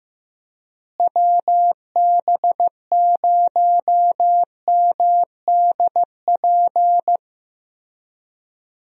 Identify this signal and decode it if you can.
Morse code, heard as a keyed tone, about 16 wpm, 700 Hz. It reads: WB0MDP
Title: USA Call Signs - 6 Characters - Repeated 3x - 15wpm